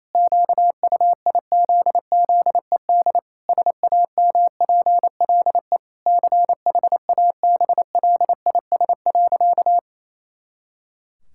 Morse code, heard as a keyed tone, about 27 wpm, 700 Hz. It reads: QUIZZED SAMPLE CHABLIS.
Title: Pangram Challenge v2 - 28wpm